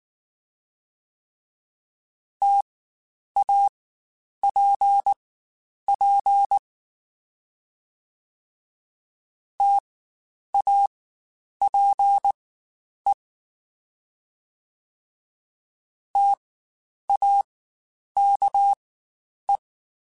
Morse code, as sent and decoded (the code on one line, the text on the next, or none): - .- .--. .--. / - .- .--. / - .- -.- .
TAPP TAPE TAKE